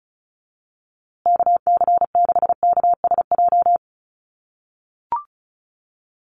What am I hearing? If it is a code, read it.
Morse code, heard as a keyed tone, about 34 wpm, 700 Hz. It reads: KC6KSJ
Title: USA Call Signs Repeated 3x 35wpm